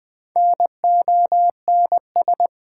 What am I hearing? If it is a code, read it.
Morse code, heard as a keyed tone, about 20 wpm, 700 Hz. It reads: NONS
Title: Build A Word - 2 Letters At A Time - 8-Letter Words - 20wpm